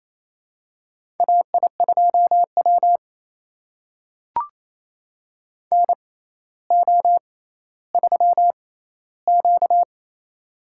Morse code, heard as a keyed tone, about 28 wpm, spaced as for 14 wpm, 700 Hz. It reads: AI2W NO3Q
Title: USA Call Signs - 4 Characters - 6x Character Spacing - 28wpm